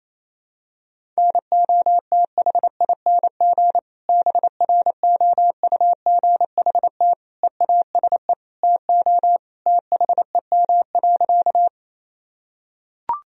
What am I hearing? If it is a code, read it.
Morse code, heard as a keyed tone, about 28 wpm, 700 Hz. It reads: NOTHING BROUGHT EASE TO THEM.
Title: Sentences from Top 1000 Words - Review Part 1 - 28wpm